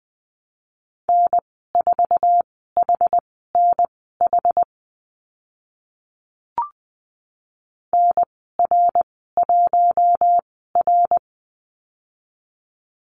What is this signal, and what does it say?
Morse code, heard as a keyed tone, about 20 wpm, spaced as for 14 wpm, 700 Hz. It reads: N4HNH NR1R
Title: 076 Lesson 13g - Letter L - Call Signs 20wpm